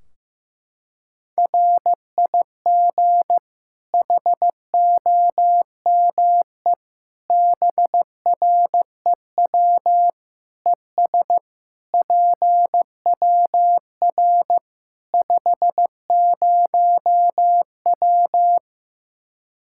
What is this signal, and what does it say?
Morse code, heard as a keyed tone, about 15 wpm, 700 Hz. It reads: RIG HOME BREW ES PWR 50W
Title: Practice QSO - Exchange 2 - Info - 15wpm